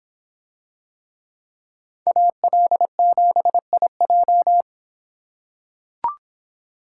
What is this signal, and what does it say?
Morse code, heard as a keyed tone, about 26 wpm, 700 Hz. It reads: AL7IJ